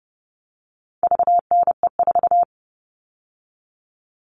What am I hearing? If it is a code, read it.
Morse code, heard as a keyed tone, about 30 wpm, 700 Hz. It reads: VNE4
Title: All Cut Numbers - 4-Digit 30wpm